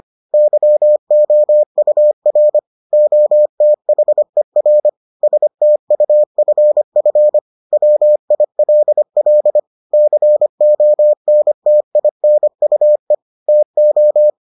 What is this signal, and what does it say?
Morse code, heard as a keyed tone, about 25 wpm, 600 Hz. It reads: YOUR OTHER STUFF WILL CONTINUE TO